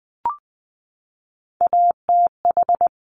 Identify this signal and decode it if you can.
Morse code, heard as a keyed tone, about 20 wpm, 700 Hz. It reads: ATH
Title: Top Three Letter Combinations in Words 20wpm